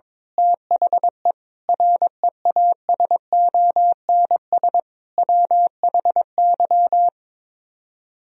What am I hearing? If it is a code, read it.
Morse code, heard as a keyed tone, about 22 wpm, 700 Hz. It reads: THE REASONS WHY
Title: Speed-Racing - Sets of 5 Words 22wpm